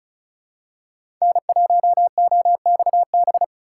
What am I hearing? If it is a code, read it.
Morse code, heard as a keyed tone, about 36 wpm, 690 Hz. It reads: N1OXB
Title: USA Call Signs - 5 Characters - 3x Character Spacing - 35wpm